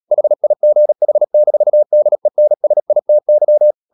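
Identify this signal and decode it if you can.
Morse code, heard as a keyed tone, about 37 wpm, 600 Hz. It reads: HIGH-DENSITY
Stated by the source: Morse code practice at 35–39 wpm sending English text of Ken Follett's novel Never, from CW med Gnister.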